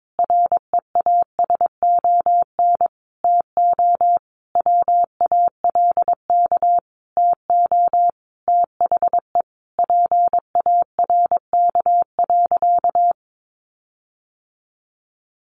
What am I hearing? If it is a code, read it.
Morse code, heard as a keyed tone, about 22 wpm, 700 Hz. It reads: REASON TO WALK TO THE PARK.
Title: Sentences from Top 1400 Words 22wpm